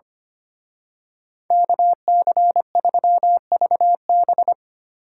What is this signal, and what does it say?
Morse code, heard as a keyed tone, about 29 wpm, 700 Hz. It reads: KC3VB
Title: Speed Racing - USA 2x2 Call Signs 25wpm